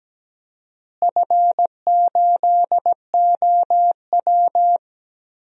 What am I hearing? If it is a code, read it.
Morse code, heard as a keyed tone, about 17 wpm, 700 Hz. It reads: F8OW
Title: DXCC-100 Call Signs - 4 Characters - Repeated 3x - 17wpm